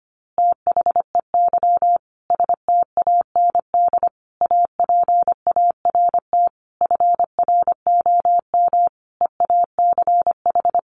Morse code, heard as a keyed tone, about 25 wpm, 700 Hz. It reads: THEY STAND APART FROM EACH
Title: Easy Phrases 25wpm